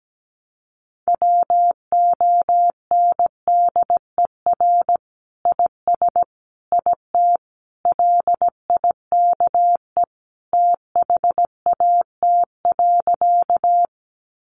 Morse code, heard as a keyed tone, about 17 wpm, 700 Hz. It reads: WONDER IS IT LIKE THAT.
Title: Sentences from Top 500 Words 17wpm